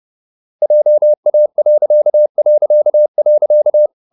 Morse code, heard as a keyed tone, about 30 wpm, 600 Hz. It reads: JA...
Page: Ferie 30 wpm